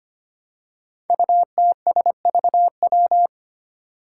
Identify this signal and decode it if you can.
Morse code, heard as a keyed tone, about 25 wpm, 700 Hz. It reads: UTSVW
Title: Five Challenging Letter-Number Combinations 25wpm